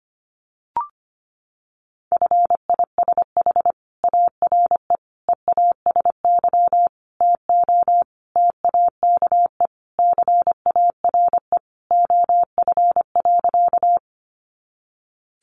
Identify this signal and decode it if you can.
Morse code, heard as a keyed tone, about 25 wpm, 700 Hz. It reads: FISH ARE EASY TO TAKE CARE OF.
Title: Additional Easy Phrases 25wpm